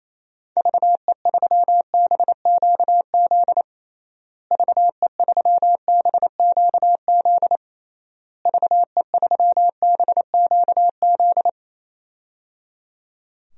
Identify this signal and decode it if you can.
Morse code, heard as a keyed tone, about 28 wpm, 700 Hz. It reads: VE3BQZ VE3BQZ VE3BQZ
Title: DXCC-100 Call Signs - 6 Characters - Repeated 3x - 28wpm